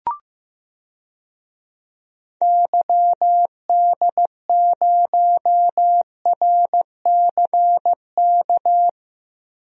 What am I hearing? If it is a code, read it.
Morse code, heard as a keyed tone, about 15 wpm, 700 Hz. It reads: YD0RCK
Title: DXCC-100 Call Signs - 6 Characters - Repeated 3x - 15wpm